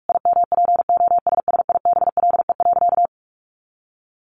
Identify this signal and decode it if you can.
Morse code, heard as a keyed tone, about 44 wpm, 700 Hz. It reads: IMPOSSIBLE.
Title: Sentences from Top 1100 Words 45wpm